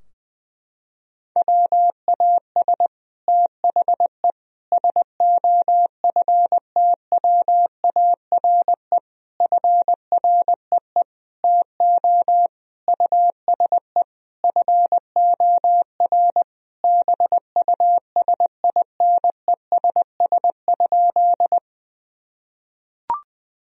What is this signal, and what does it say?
Morse code, heard as a keyed tone, about 20 wpm, 700 Hz. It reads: WAS THE SOFTWARE FREE TO USE FOR BUSINESS?
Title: Sentences from Top 1000 Words - Review Part 3 - 20wpm